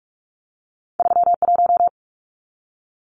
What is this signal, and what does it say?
Morse code, heard as a keyed tone, about 46 wpm, 700 Hz. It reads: 31
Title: Numbers - 2-Digit 45wpm